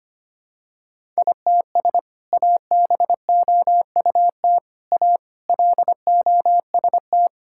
Morse code, heard as a keyed tone, about 25 wpm, 700 Hz. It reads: ITS ABOUT A LOST
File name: practice / 25wpm / Sentences with 2-Word Phrases - 25wpm